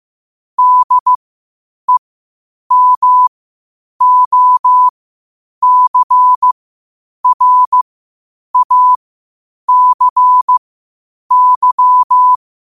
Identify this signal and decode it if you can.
Morse code, heard as a keyed tone, about 15 wpm, spaced as for 8 wpm, 1000 Hz. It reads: DEMOCRACY
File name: 8wpm